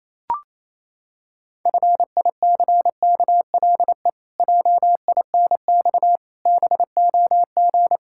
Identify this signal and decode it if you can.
Morse code, heard as a keyed tone, about 28 wpm, 700 Hz. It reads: FICKLE JINX BOG
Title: Pangram Challenge v2 - 28wpm